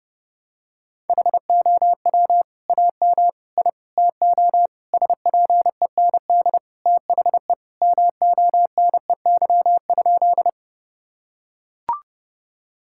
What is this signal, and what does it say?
Morse code, heard as a keyed tone, about 30 wpm, 700 Hz. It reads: HOW AM I TO SPEND THE MONEY?
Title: Sentences from Top 1000 Words - Review Part 5 - 30wpm